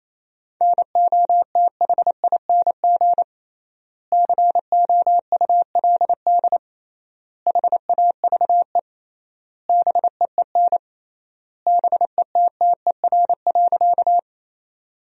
Morse code, heard as a keyed tone, about 28 wpm, 700 Hz. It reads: NOTHING COULD HAVE BEEN BETTER.
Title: 3x Word-Spacing - Sentences from Top 500 Words - 28wpm